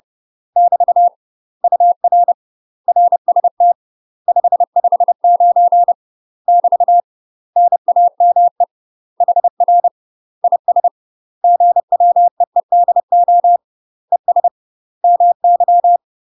Morse code, heard as a keyed tone, about 30 wpm, 700 Hz. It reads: <BT> UR RST 559 <BT> NAME HR IS GWEEDO ES MY
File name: UK MASTER QSOs TRAINING / 30WPM / UK MASTER QSOs 0005